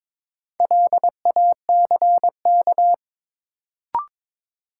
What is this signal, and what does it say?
Morse code, heard as a keyed tone, about 22 wpm, 700 Hz. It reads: LACK
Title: Sets of 3 Words - New Word Drill - 22wpm